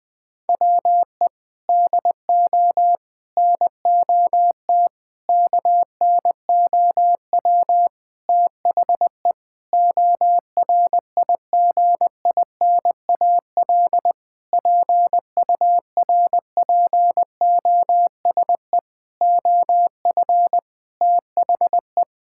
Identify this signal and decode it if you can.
Morse code, heard as a keyed tone, about 20 wpm, 700 Hz. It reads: WE DO NOT KNOW THE ORIGINAL PURPOSE OF THE
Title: Sentences from Top 1000 Words - Review Part 6 - 20wpm